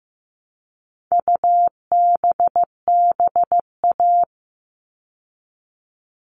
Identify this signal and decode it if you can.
Morse code, heard as a keyed tone, about 15 wpm, 700 Hz. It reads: UBBA
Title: All Cut Numbers - 4-Digit 15wpm